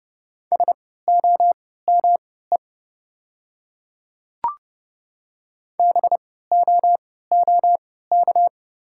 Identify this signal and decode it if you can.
Morse code, heard as a keyed tone, about 30 wpm, spaced as for 17 wpm, 690 Hz. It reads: SOME BOOK
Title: 3x Character Spacing - Top 300 Words - 30wpm